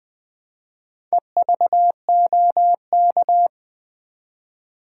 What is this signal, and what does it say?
Morse code, heard as a keyed tone, about 20 wpm, 700 Hz. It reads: EVOK
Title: Build A Word - 1 Letter At A Time - 5-Letter Words - 20wpm